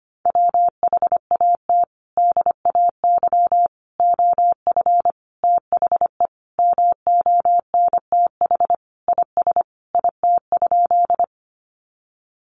Morse code, heard as a keyed tone, about 25 wpm, 700 Hz. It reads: WHAT DAY OF THE MONTH IS IT?
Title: Sentences from Top 800 Words 25wpm